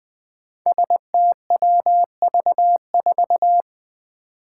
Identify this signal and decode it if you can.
Morse code, heard as a keyed tone, about 20 wpm, 700 Hz. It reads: STWV4
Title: Five Challenging Letter-Number Combinations 20wpm